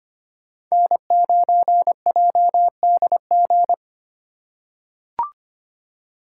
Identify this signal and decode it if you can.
Morse code, heard as a keyed tone, about 25 wpm, 700 Hz. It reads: N9JDG
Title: USA Call Signs - 5 Characters - 8x Character Spacing - 25wpm